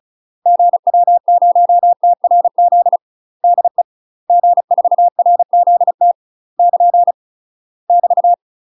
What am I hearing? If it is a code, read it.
Morse code, heard as a keyed tone, about 36 wpm, 700 Hz. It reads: GW0TRZ DE G4RZT <KN> <BT>
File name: UK MASTER QSOs TRAINING / 35WPM / UK MASTER QSOs 0041